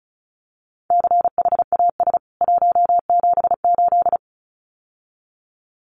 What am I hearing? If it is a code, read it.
Morse code, heard as a keyed tone, about 34 wpm, 700 Hz. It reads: CHAS 178
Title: CWT Exchange 35wpm